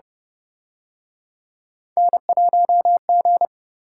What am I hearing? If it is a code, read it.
Morse code, heard as a keyed tone, about 30 wpm, 700 Hz. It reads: N1G